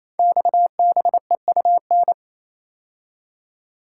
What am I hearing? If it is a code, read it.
Morse code, heard as a keyed tone, about 29 wpm, 690 Hz. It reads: XBEUN
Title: Code Groups - 5 Letters - 28wpm